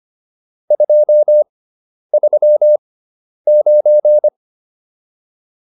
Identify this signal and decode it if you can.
Morse code, heard as a keyed tone, about 25 wpm, spaced as for 10 wpm, 600 Hz. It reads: 239